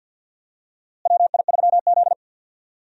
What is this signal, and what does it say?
Morse code, heard as a keyed tone, about 50 wpm, 700 Hz. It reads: WI3Z
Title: USA Call Signs - 4 Characters - Repeated 3x - 50wpm